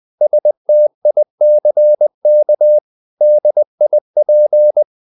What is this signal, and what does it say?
Morse code, heard as a keyed tone, about 20 wpm, 600 Hz. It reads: STICK DIP
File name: the old man and the sea / the old man and the sea part 1 20wpm